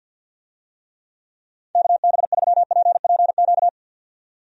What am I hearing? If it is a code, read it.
Morse code, heard as a keyed tone, about 51 wpm, 700 Hz. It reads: KB3PPQ
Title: USA Call Signs - 6 Characters - 50wpm